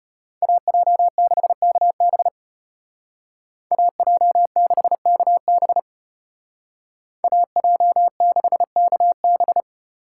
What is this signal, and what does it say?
Morse code, heard as a keyed tone, about 33 wpm, 700 Hz. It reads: AJ6KB AJ6KB AJ6KB